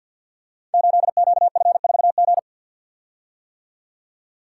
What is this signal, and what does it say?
Morse code, heard as a keyed tone, about 46 wpm, 700 Hz. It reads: 8QF4G
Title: Code Groups - 5 Letter:Numbers - 50wpm